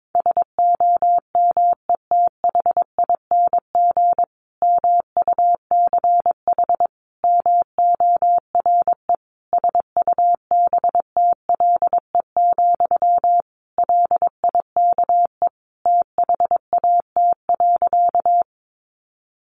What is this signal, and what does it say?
Morse code, heard as a keyed tone, about 22 wpm, 700 Hz. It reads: SOMETHING MUCH MORE SUBTLE, LIKE THAT.
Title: A Sound Of Thunder 22wpm